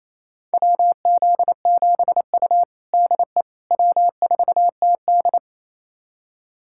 Morse code, heard as a keyed tone, about 28 wpm, 700 Hz. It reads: WZ7U DE W4TD
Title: USA Callsign DE Callsign - 4 Characters - 28wpm